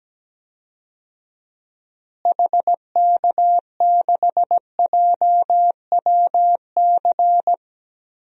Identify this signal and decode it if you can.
Morse code, heard as a keyed tone, about 17 wpm, 700 Hz. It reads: HK6JWC
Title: DXCC-100 Call Signs - 6 Characters - Repeated 3x - 17wpm